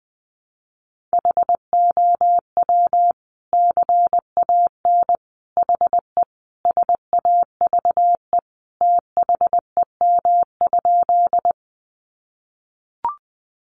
Morse code, heard as a keyed tone, about 20 wpm, 700 Hz. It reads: HOW CAN HE SAVE THEM?
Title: Sentences from Top 1100 Words 20wpm